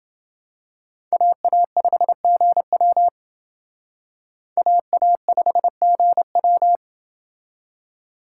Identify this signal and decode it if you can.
Morse code, heard as a keyed tone, about 28 wpm, 700 Hz. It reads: AA5GW AA5GW